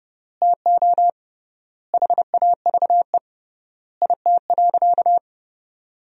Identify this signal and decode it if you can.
Morse code, heard as a keyed tone, about 30 wpm, 700 Hz. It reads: TO HAVE IT.
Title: 3x Word-Spacing - Sentences from Top 100 Words - 30wpm